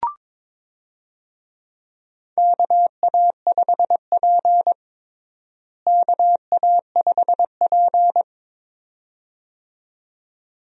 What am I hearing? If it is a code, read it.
Morse code, heard as a keyed tone, about 22 wpm, 700 Hz. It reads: KA5P KA5P
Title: USA Call Signs - 4 Characters - Repeated 2x - 22wpm